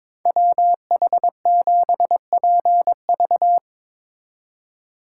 Morse code, heard as a keyed tone, about 24 wpm, 700 Hz. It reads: WH7PV
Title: Speed Racing - USA 2x2 Call Signs 22wpm